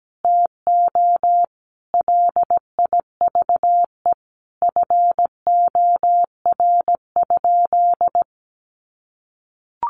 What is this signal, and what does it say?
Morse code, heard as a keyed tone, about 17 wpm, 700 Hz. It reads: TO LIVE FOR?
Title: Sentences from Top 200 Words 17wpm